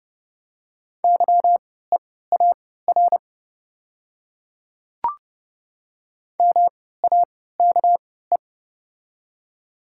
Morse code, heard as a keyed tone, about 30 wpm, spaced as for 17 wpm, 690 Hz. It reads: YEAR MAKE
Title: 3x Character Spacing - Top 500 Words - 30wpm